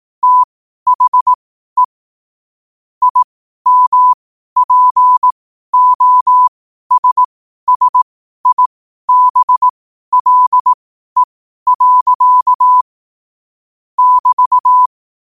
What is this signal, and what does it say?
Morse code, heard as a keyed tone, about 18 wpm, spaced as for 12 wpm, 1000 Hz. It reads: THE IMPOSSIBLE. <BT>
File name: qotdfiles / 12wpm